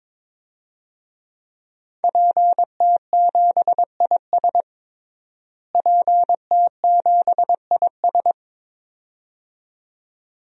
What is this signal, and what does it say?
Morse code, heard as a keyed tone, about 21 wpm, 700 Hz. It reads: PT7IS PT7IS
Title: DXCC-100 Call Signs - 5 Characters - Repeated 2x - 22wpm